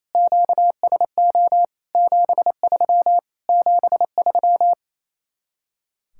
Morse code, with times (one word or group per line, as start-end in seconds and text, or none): 0.1-1.6 QSO
1.9-3.2 73
3.5-4.7 73